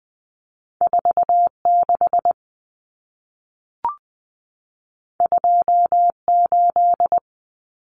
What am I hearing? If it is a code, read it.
Morse code, heard as a keyed tone, about 20 wpm, 700 Hz. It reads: 46 28
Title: Common Cut Numbers - 2-Digit 20wpm